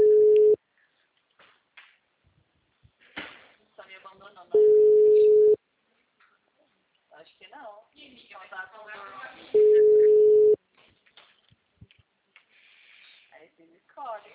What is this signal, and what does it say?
Ringback tone (Europe, 425 Hz), with speech.